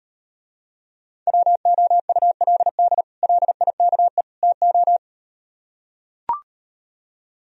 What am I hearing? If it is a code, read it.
Morse code, heard as a keyed tone, about 38 wpm, 700 Hz. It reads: WOULD LIKE TO